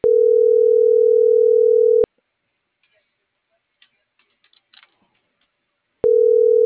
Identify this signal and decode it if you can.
Ringback tone (North America), 440 + 480 Hz.